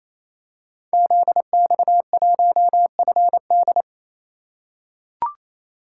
Morse code, heard as a keyed tone, about 28 wpm, 700 Hz. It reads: ZX1FD
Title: Code Groups - 5 Letter:Numbers - 28wpm